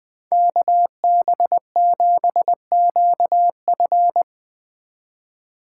Morse code, heard as a keyed tone, about 20 wpm, 700 Hz. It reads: KB7QF
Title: Speed Racing - USA Call Signs 17wpm